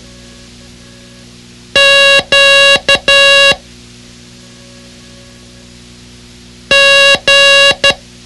Morse code, heard as a keyed tone, about 12 wpm, 560 Hz. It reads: Q G
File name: ham radio / CodeQuick / Code Quick 06